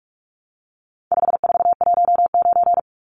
Morse code, heard as a keyed tone, about 44 wpm, 700 Hz. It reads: 5419
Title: Numbers - 4-Digit 45wpm